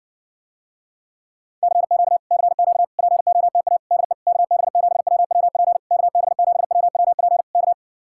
Morse code, heard as a keyed tone, about 60 wpm, 700 Hz. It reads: CQ CQ POTA DE KB7WWP KB7WWP K